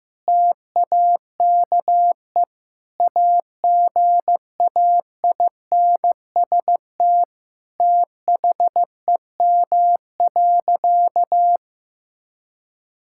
Morse code, heard as a keyed tone, about 15 wpm, 700 Hz. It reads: TAKE AGAINST THEM.
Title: Sentences from Top 1000 Words - Review Part 6 - 15wpm